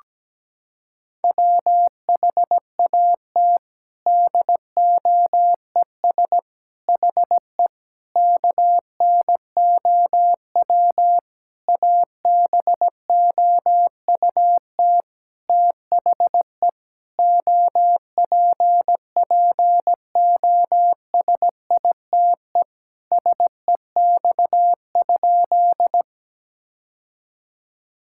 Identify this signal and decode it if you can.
Morse code, heard as a keyed tone, about 17 wpm, 700 Hz. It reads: WHAT DOES HE KNOW ABOUT THE OPPOSITE SEX?
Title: Sentences from Top 1600 Words 17wpm